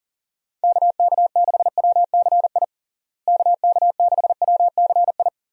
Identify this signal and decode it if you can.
Morse code, heard as a keyed tone, about 40 wpm, 700 Hz. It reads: KK6WCI KK6WCI
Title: USA Call Signs - 6 Characters - Repeated 2x - 40wpm